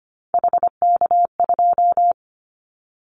Morse code, heard as a keyed tone, about 25 wpm, 700 Hz. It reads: HK2
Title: Callsign Prefixes for Top 100 DXCC - 25wpm